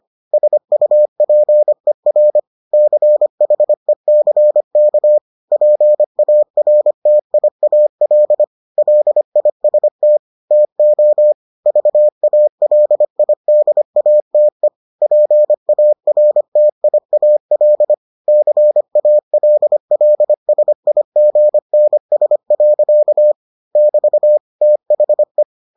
Morse code, heard as a keyed tone, about 25 wpm, 600 Hz. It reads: SUPER CHECK PARTIAL LIST TO VALIDATE PARTIAL CALLSIGNS. <BT> THE